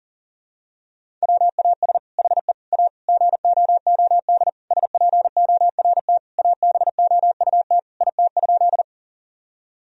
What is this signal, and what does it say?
Morse code, heard as a keyed tone, about 37 wpm, 700 Hz. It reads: WAS HE A GOOD SPORT ABOUT IT?
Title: Sentences from Top 1000 Words - Review Part 4 - 40wpm